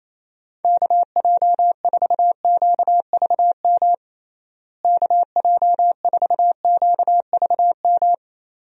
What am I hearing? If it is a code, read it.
Morse code, heard as a keyed tone, about 28 wpm, 700 Hz. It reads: KJ4QVM KJ4QVM
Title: USA Call Signs - 6 Characters - Repeated 2x - 28wpm